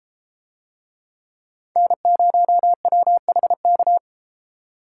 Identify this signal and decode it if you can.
Morse code, heard as a keyed tone, about 32 wpm, 700 Hz. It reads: N0WHK